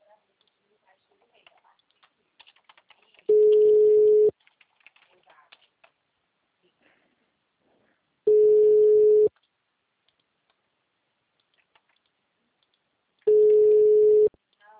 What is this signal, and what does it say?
Ringback tone (Europe, 425 Hz), with speech.
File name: out- -2002- - - .12841